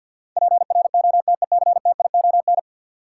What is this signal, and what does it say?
Morse code, heard as a keyed tone, about 53 wpm, 700 Hz. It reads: PROTECTION
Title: Sentences from Top 1600 Words - New Word Drill Alt - 50wpm